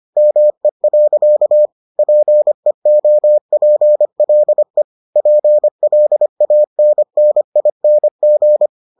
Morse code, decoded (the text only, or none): ME. PEOPLE PLANNING